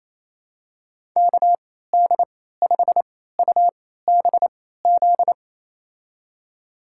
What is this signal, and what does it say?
Morse code, heard as a keyed tone, about 27 wpm, spaced as for 16 wpm, 700 Hz. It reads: KD5UBZ
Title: USA Call Signs - 3x Character Spacing - 28wpm